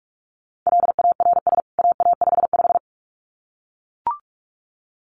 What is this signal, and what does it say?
Morse code, heard as a keyed tone, about 44 wpm, 700 Hz. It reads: LARS AA55
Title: CWT Exchange 45wpm